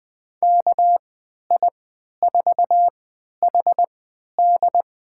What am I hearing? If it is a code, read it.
Morse code, heard as a keyed tone, about 20 wpm, spaced as for 11 wpm, 700 Hz. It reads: KI4HD
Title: USA Call Signs - 3x Character Spacing - 20wpm